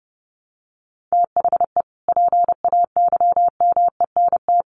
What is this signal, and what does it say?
Morse code, heard as a keyed tone, about 28 wpm, 700 Hz. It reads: THE PAYMENT
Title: Sentences from Top 1200 Words 30wpm